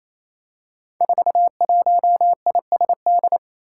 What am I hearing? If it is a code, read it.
Morse code, heard as a keyed tone, about 28 wpm, 700 Hz. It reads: 41ISD
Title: Five Challenging Letter-Number Combinations 28wpm